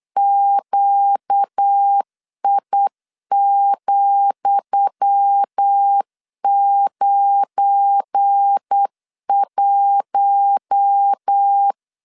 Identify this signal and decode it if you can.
Morse code, heard as a keyed tone, about 8 wpm, 800 Hz. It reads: QI,91
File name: LEZIONI / 10 SIGNS / 10D COMA 8wpm 3749